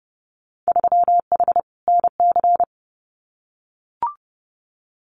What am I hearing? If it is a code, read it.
Morse code, heard as a keyed tone, about 31 wpm, 700 Hz. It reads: 3H NC
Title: Winter Field Day Exchanges 30wpm